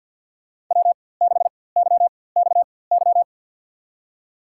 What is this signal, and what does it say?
Morse code, heard as a keyed tone, about 52 wpm, spaced as for 23 wpm, 700 Hz. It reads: W6YXY
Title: USA Call Signs - 5 Characters - 4x Character Spacing - 50wpm